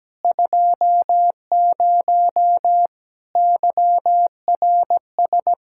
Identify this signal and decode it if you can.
Morse code, heard as a keyed tone, about 17 wpm, 700 Hz. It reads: 20 YRS
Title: Practice QSO - Exchange 3 - More Info - 17wpm